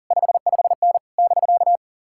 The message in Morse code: 55N <BK>

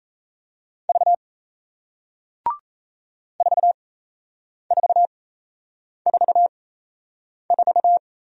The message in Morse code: V 4444

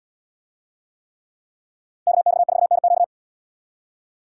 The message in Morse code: KX4NQ

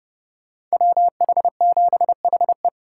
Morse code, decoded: WH7HE

